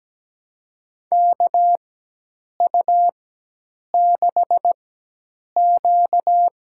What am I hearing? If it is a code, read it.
Morse code, heard as a keyed tone, about 17 wpm, spaced as for 8 wpm, 700 Hz. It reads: KU6Q
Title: USA Call Signs - 4 Characters - 4x Character Spacing - 17wpm